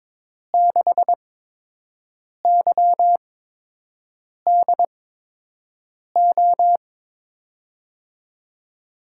Morse code, heard as a keyed tone, about 22 wpm, spaced as for 6 wpm, 700 Hz. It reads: 6YDO
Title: USA Call Signs - 6 Characters - 8x Character Spacing - 22wpm